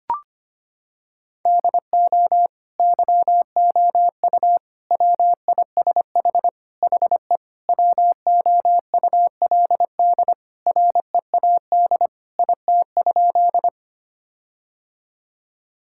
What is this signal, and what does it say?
Morse code, heard as a keyed tone, about 25 wpm, 700 Hz. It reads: DO YOU WISH HE WOULD READ IT?
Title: Sentences from Top 1600 Words 25wpm